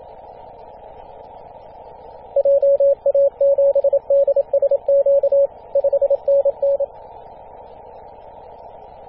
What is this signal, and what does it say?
Morse code, heard as a keyed tone, about 27 wpm, 560 Hz. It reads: JA7DSQ 5NN
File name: posts / 2016-feb-27-0448z-a35t-24-898mhz-cw